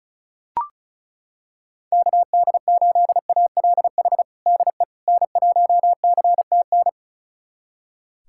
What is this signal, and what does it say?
Morse code, heard as a keyed tone, about 34 wpm, 700 Hz. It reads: KD8ALH DE N1CTN
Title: USA Callsign DE Callsign 35wpm